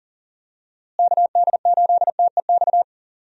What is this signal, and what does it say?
Morse code, heard as a keyed tone, about 40 wpm, 700 Hz. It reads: KD8TEX
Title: USA Call Signs - 6 Characters - Repeated 3x - 40wpm